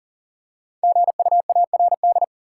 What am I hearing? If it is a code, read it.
Morse code, heard as a keyed tone, about 40 wpm, 700 Hz. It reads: GUARD
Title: Practice QSO - Exchange 3 - More Info - 40wpm